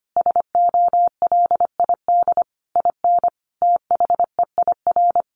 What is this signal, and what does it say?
Morse code, heard as a keyed tone, about 25 wpm, 700 Hz. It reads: SOLID IN THEIR
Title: Sentences from Top 1500 Words 25wpm